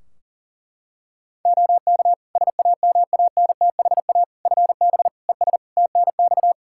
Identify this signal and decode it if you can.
Morse code, heard as a keyed tone, about 40 wpm, 700 Hz. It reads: OK SAMANTHA FB ES TNX
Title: Practice QSO - Exchange 2 - Info - 40wpm